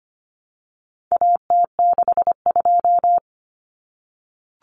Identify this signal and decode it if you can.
Morse code, heard as a keyed tone, about 25 wpm, 700 Hz. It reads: AT62
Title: Common Cut Numbers - 4-Digit 25wpm